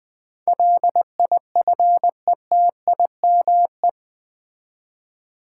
Morse code, heard as a keyed tone, about 20 wpm, 700 Hz. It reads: LIFETIME
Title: 8 Letter Words - First 3 Letters Given - 20wpm